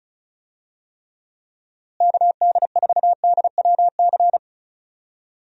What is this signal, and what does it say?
Morse code, heard as a keyed tone, about 36 wpm, 700 Hz. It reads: KD4DWC
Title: USA Call Signs - 6 Characters - Repeated 3x - 35wpm